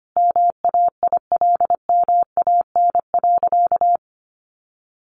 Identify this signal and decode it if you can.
Morse code, heard as a keyed tone, about 25 wpm, 700 Hz. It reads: MAILMAN.
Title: Additional Easy Phrases 25wpm